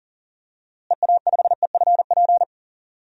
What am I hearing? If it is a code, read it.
Morse code, heard as a keyed tone, about 37 wpm, 700 Hz. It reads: EA5EFP
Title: DXCC-100 Call Signs - 6 Characters - Repeated 2x - 40wpm